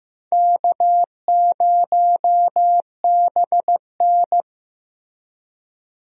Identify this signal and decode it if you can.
Morse code, heard as a keyed tone, about 15 wpm, 700 Hz. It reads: K0BN
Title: USA Call Signs - 4 Characters - Repeated 3x - 15wpm